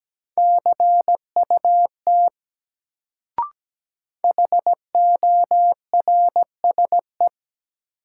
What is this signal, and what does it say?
Morse code, heard as a keyed tone, about 17 wpm, 700 Hz. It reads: CUT HORSE
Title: Sentences from Top 300 Words - New Word Drill - 17wpm